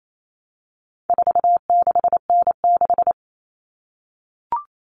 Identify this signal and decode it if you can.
Morse code, heard as a keyed tone, about 28 wpm, 700 Hz. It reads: 46N6
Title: Common Cut Numbers - 4-Digit 28wpm